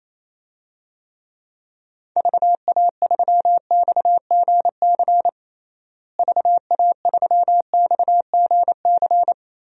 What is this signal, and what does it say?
Morse code, heard as a keyed tone, about 28 wpm, 700 Hz. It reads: VA3XGC VA3XGC
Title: DXCC-100 Call Signs - 6 Characters - Repeated 3x - 28wpm